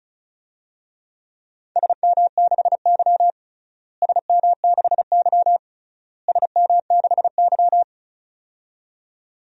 Morse code, heard as a keyed tone, about 36 wpm, 700 Hz. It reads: SM6Y SM6Y SM6Y
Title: DXCC-100 Call Signs - 4 Characters - Repeated 3x - 35wpm